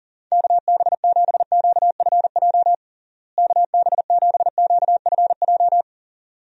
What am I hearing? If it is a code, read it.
Morse code, heard as a keyed tone, about 40 wpm, 700 Hz. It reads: KB7QFJ KB7QFJ
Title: USA Call Signs - 6 Characters - Repeated 2x - 40wpm